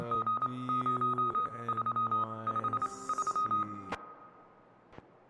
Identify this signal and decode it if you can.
Morse code, heard as a keyed tone, about 27 wpm, 1250 Hz. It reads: G 0 3 5 3